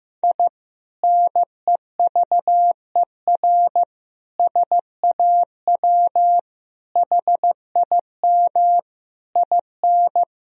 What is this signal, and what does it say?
Morse code, heard as a keyed tone, about 16 wpm, 700 Hz. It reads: I NEVER SAW HIM IN
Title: Sentences from Top 1000 Words - Review Part 2 - 15wpm